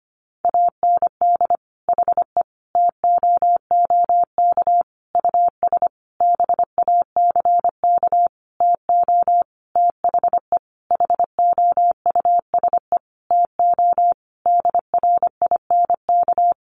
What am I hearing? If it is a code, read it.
Morse code, heard as a keyed tone, about 25 wpm, 700 Hz. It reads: AND HE TOOK US BACK TO THE HOUSE TO DRINK
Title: Sentences from Top 1500 Words 25wpm